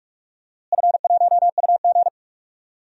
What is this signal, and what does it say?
Morse code, heard as a keyed tone, about 43 wpm, 690 Hz. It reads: F1UG
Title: DXCC-100 Call Signs - 4 Characters - Repeated 3x - 45wpm